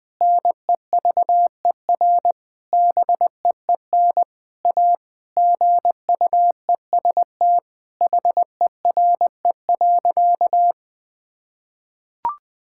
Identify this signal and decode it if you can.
Morse code, heard as a keyed tone, about 20 wpm, 700 Hz. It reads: NEVER BEEN A GUEST HERE.
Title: Sentences from Top 1600 Words 20wpm